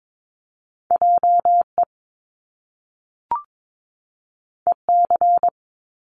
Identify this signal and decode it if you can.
Morse code, heard as a keyed tone, about 22 wpm, 700 Hz. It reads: JE EC
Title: Top Two Letter Combinations in Words 22wpm